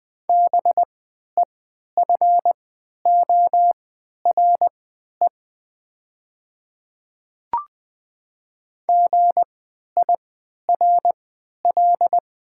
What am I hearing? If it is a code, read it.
Morse code, heard as a keyed tone, about 20 wpm, spaced as for 11 wpm, 700 Hz. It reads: BEFORE GIRL
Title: 3x Character Spacing - Top 500 Words - 20wpm